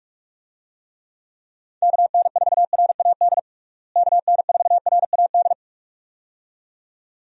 Morse code, heard as a keyed tone, about 46 wpm, 700 Hz. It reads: KN4RAD KN4RAD
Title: USA Call Signs - 6 Characters - Repeated 2x - 45wpm